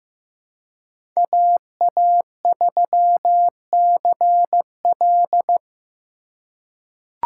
Morse code, heard as a keyed tone, about 15 wpm, 700 Hz. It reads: AA3CL
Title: USA Call Signs - 4x Character Spacing - 15wpm